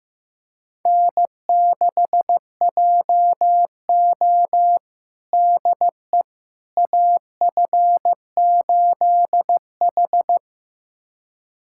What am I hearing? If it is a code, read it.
Morse code, heard as a keyed tone, about 15 wpm, 700 Hz. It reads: N6JO DE AF8H
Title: USA Callsign DE Callsign - 4 Characters - 15wpm